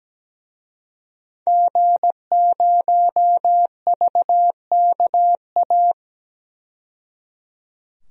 Morse code, heard as a keyed tone, about 17 wpm, 700 Hz. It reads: G0VKA
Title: DXCC-100 Call Signs - 5 Characters - 17wpm